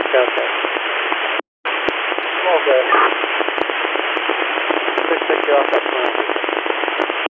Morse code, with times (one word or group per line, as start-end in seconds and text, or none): none